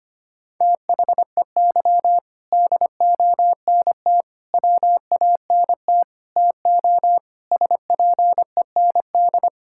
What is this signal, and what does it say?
Morse code, heard as a keyed tone, about 25 wpm, 700 Hz. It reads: THEY DONT WANT TO SPEND
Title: Sentences from Top 1000 Words - Review Part 5 - 25wpm